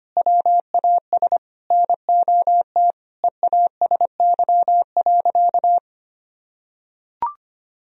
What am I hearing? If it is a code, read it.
Morse code, heard as a keyed tone, about 25 wpm, 700 Hz. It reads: WAS NOT EASY.
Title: Sentences from Top 1000 Words - Review Part 2 - 25wpm